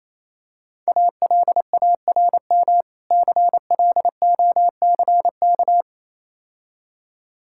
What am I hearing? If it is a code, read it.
Morse code, heard as a keyed tone, about 27 wpm, 700 Hz. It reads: ALARM CLOCK
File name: Easy Words - Speed-Racing with Hints - 22wpm